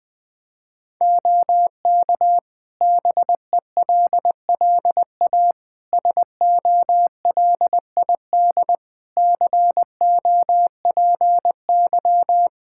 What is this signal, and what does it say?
Morse code, heard as a keyed tone, about 20 wpm, 700 Hz. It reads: OK BELLA SOLID COPY
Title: K1USN SST QSO Practice 20wpm